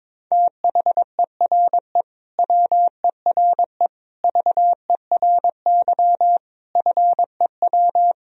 Morse code, heard as a keyed tone, about 22 wpm, 700 Hz. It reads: THERE WERE VERY FEW
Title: Sentences from Top 1000 Words - Review Part 2 - 22wpm